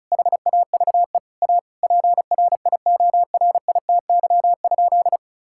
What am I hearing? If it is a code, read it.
Morse code, heard as a keyed tone, about 36 wpm, 700 Hz. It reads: HAVE A PRIORITY?
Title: Sentences from Top 1600 Words 35wpm